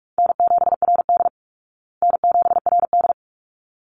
Morse code, heard as a keyed tone, about 44 wpm, 700 Hz. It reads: N7RD N7RD
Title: Easy Call Signs Repeated 2x - 45wpm